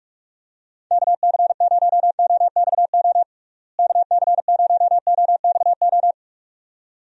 Morse code, heard as a keyed tone, about 46 wpm, 700 Hz. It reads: KC0OXO KC0OXO
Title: USA Call Signs - 6 Characters - Repeated 2x - 45wpm